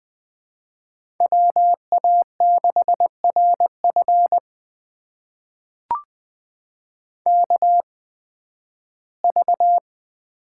Morse code, heard as a keyed tone, about 20 wpm, spaced as for 13 wpm, 700 Hz. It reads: WA6RF KV
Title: USA Call Signs - 5 Characters - 8x Character Spacing - 20wpm